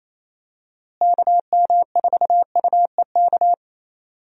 Morse code, heard as a keyed tone, about 28 wpm, 700 Hz. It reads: KM4UEK
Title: USA Call Signs - 6 Characters - Repeated 3x - 28wpm